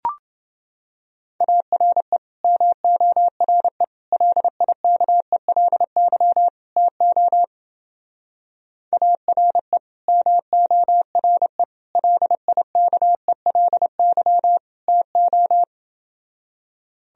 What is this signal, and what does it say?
Morse code, heard as a keyed tone, about 29 wpm, 700 Hz. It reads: ARE MORE LIKELY TO ARE MORE LIKELY TO